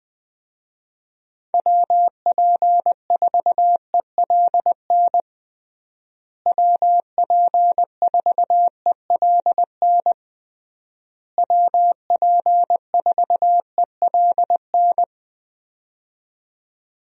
Morse code, heard as a keyed tone, about 20 wpm, 700 Hz. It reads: WP4ELN WP4ELN WP4ELN
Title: USA Call Signs - 6 Characters - Repeated 3x - 20wpm